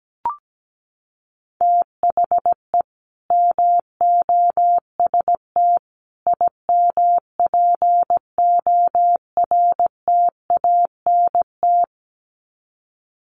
Morse code, heard as a keyed tone, about 17 wpm, 700 Hz. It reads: THE MOST IMPORTANT
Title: Sets of 3 Words 17wpm